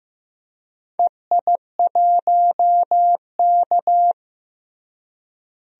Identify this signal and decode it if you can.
Morse code, heard as a keyed tone, about 15 wpm, 700 Hz. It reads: EI1K
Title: DXCC-100 Call Signs - 4 Characters - Repeated 2x - 15wpm